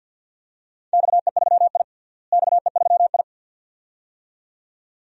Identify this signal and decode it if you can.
Morse code, heard as a keyed tone, about 52 wpm, 700 Hz. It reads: XE3I XE3I
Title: DXCC-100 Call Signs - 4 Characters - Repeated 2x - 50wpm